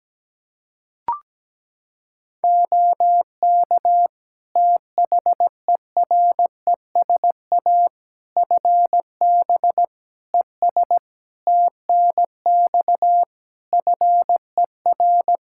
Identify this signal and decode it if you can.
Morse code, heard as a keyed tone, about 17 wpm, 700 Hz. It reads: OK THERESA FB ES TNX FER
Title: Practice QSO - Exchange 2 - Info - 17wpm